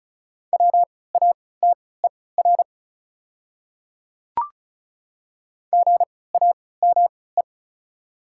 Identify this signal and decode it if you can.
Morse code, heard as a keyed tone, about 36 wpm, spaced as for 20 wpm, 700 Hz. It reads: WATER GAME
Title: 3x Character Spacing - Top 500 Words - 35wpm